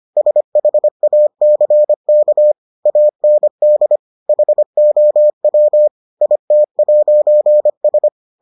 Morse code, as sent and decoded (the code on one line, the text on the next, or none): ... .... .- -.-. -.- / .- -. -.. / .... --- .-- / .. - .----. ...
SHACK AND HOW IT'S